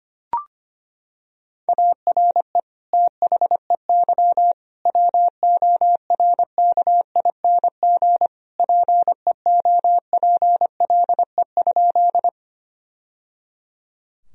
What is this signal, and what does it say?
Morse code, heard as a keyed tone, about 25 wpm, 700 Hz. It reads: ARE THEY WORKING PEOPLE?
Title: Sentences from Top 1000 Words - Review Part 4 - 25wpm